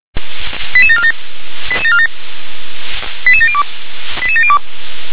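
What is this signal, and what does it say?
Five-tone selective call (ZVEI), heard as quick runs of five tones, tones of about 80 ms. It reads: 88747 88747 88702 88702
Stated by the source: ZVEI five-tone call received by SDR.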